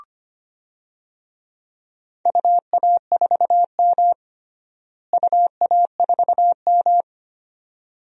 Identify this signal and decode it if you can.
Morse code, heard as a keyed tone, about 25 wpm, 700 Hz. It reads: UA4M UA4M
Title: DXCC-100 Call Signs - 4 Characters - Repeated 2x - 25wpm